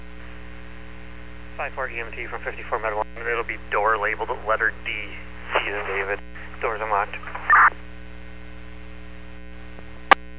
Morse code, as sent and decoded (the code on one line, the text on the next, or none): none